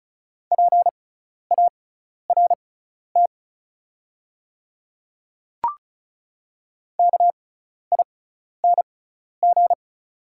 Morse code, heard as a keyed tone, about 34 wpm, spaced as for 12 wpm, 700 Hz. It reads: PART KING
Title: 6x Character Spacing - Top 500 Words - 35wpm